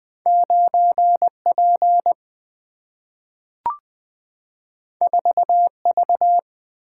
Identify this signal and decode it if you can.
Morse code, heard as a keyed tone, about 20 wpm, 700 Hz. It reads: 9 P 4 V
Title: Single-Letter Single-Number 20wpm